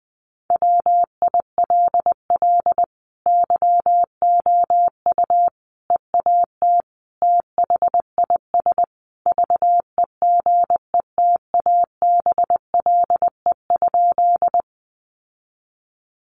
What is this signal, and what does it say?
Morse code, heard as a keyed tone, about 20 wpm, 700 Hz. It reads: WILL YOU EAT THIS VEGETABLE?
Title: Sentences from Top 1300 Words 20wpm